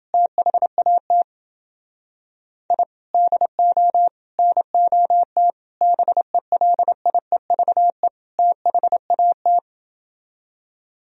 Morse code, heard as a keyed tone, about 28 wpm, 700 Hz. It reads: THAT I DO NOT BELIEVE THAT